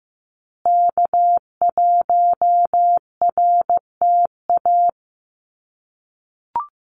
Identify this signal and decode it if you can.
Morse code, heard as a keyed tone, about 15 wpm, 700 Hz. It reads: K1RTA
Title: USA Call Signs Repeated 2x 15wpm